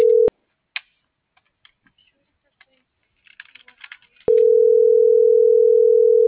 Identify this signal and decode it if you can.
Ringback tone (North America), 440 + 480 Hz.